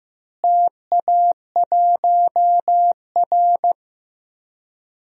Morse code, heard as a keyed tone, about 15 wpm, 700 Hz. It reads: TA1R
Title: DXCC-100 Call Signs - 4 Characters - Repeated 2x - 15wpm